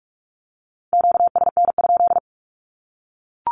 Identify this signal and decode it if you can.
Morse code, heard as a keyed tone, about 44 wpm, 700 Hz. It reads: QSN?
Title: Q-Codes 45wpm